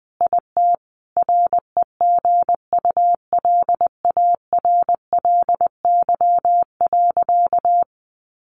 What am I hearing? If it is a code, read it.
Morse code, heard as a keyed tone, about 20 wpm, 700 Hz. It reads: IT REGULARLY.
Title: Sentences from Top 1400 Words 20wpm